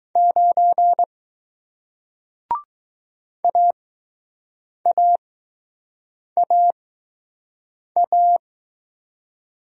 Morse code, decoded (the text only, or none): 9 AAAA